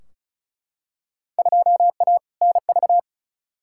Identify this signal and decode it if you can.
Morse code, heard as a keyed tone, about 36 wpm, 700 Hz. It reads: 2A NV
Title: Field Day Exchanges - 35wpm